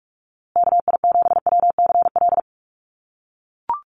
Morse code, heard as a keyed tone, about 44 wpm, 700 Hz. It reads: KI7WCL
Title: USA Call Signs Repeated 3x 45wpm